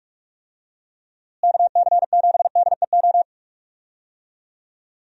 Morse code, heard as a keyed tone, about 46 wpm, 700 Hz. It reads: KC7DEO
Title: USA Call Signs - 6 Characters - Repeated 2x - 45wpm